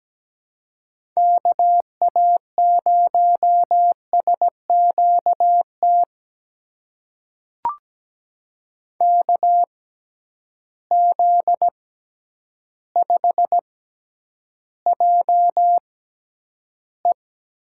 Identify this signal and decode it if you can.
Morse code, heard as a keyed tone, about 17 wpm, spaced as for 9 wpm, 700 Hz. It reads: KA0SQT KZ5JE
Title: USA Call Signs - 6 Characters - 6x Character Spacing - 17wpm